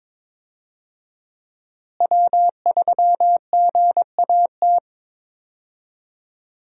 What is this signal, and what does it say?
Morse code, heard as a keyed tone, about 22 wpm, 700 Hz. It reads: W3GAT
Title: USA Call Signs - 5 Characters - Repeated 3x - 22wpm